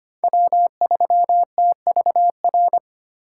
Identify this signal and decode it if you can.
Morse code, heard as a keyed tone, about 25 wpm, 700 Hz. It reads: W3TVR
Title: USA Call Signs - 5 Characters - 2x Character Spacing - 25wpm